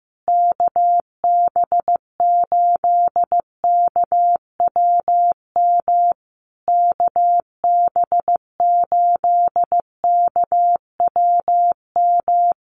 Morse code, heard as a keyed tone, about 14 wpm, 700 Hz. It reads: KB8KWM KB8KWM
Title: USA Call Signs Repeated 3x 15wpm